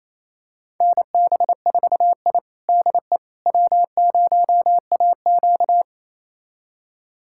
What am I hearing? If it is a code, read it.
Morse code, heard as a keyed tone, about 28 wpm, 700 Hz. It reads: NB4I DE W0AQ
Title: USA Callsign DE Callsign - 4 Characters - 28wpm